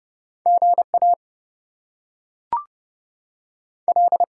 Morse code, heard as a keyed tone, about 30 wpm, 700 Hz. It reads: GA L
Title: US States and Canadian Provinces 30wpm